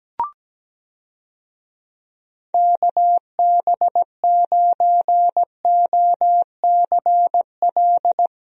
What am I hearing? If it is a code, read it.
Morse code, heard as a keyed tone, about 17 wpm, 700 Hz. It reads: KB9OCL
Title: USA Call Signs - 6 Characters - Repeated 3x - 17wpm